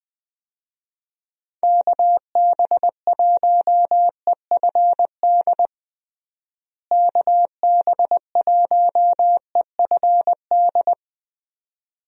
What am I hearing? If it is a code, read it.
Morse code, heard as a keyed tone, about 20 wpm, 700 Hz. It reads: KB1EFD KB1EFD
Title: USA Call Signs - 6 Characters - Repeated 2x - 20wpm